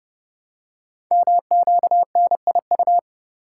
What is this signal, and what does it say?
Morse code, heard as a keyed tone, about 30 wpm, 700 Hz. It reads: MQNIU
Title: Code Groups - 5 Letters - 30wpm